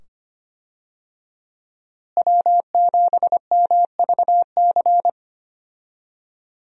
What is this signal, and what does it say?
Morse code, heard as a keyed tone, about 25 wpm, 700 Hz. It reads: W7MVC
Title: USA Call Signs - 5 Characters - Repeated 2x - 25wpm